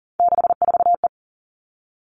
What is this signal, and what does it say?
Morse code, heard as a keyed tone, about 40 wpm, 700 Hz. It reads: 64E
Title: All Cut Numbers - 3-Digit 40wpm